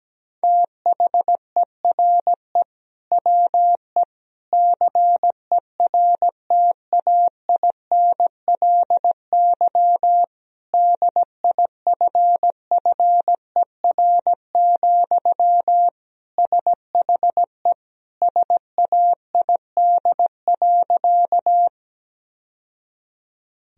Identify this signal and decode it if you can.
Morse code, heard as a keyed tone, about 17 wpm, 700 Hz. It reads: THERE WE CERTAINLY DIFFER, SHE SAID.
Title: Sentences from Top 1000 Words - Review Part 1 - 17wpm